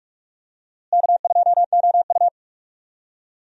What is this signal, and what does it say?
Morse code, heard as a keyed tone, about 46 wpm, 700 Hz. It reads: K2OU
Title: USA Call Signs - 4 Characters - Repeated 2x - 45wpm